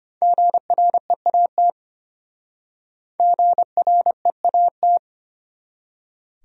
Morse code, heard as a keyed tone, about 32 wpm, 700 Hz. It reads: GREAT GREAT